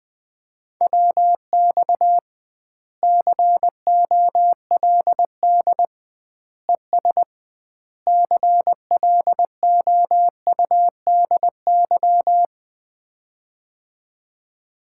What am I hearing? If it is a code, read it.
Morse code, heard as a keyed tone, about 20 wpm, 700 Hz. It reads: WX COLD ES CLOUDY
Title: Common QSO Phrases - 2x Word-Spacing - 20wpm